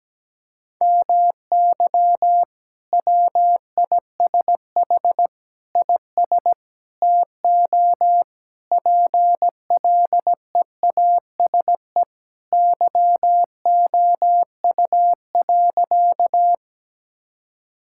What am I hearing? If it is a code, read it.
Morse code, heard as a keyed tone, about 17 wpm, 700 Hz. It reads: MY WISH IS TO PLEASE YOU.
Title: Sentences from Top 1600 Words 17wpm